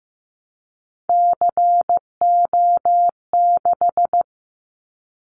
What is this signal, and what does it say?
Morse code, heard as a keyed tone, about 15 wpm, 700 Hz. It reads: CO6
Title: Callsign Prefixes for Top 100 DXCC - 15wpm